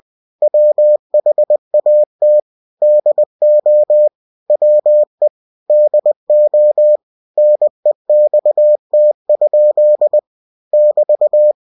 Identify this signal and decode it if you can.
Morse code, heard as a keyed tone, about 20 wpm, 600 Hz. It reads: WHAT DO WE DO NEXT? <BT>
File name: cw-books / winnie-the-pooh / winnie-20wpm-page-0076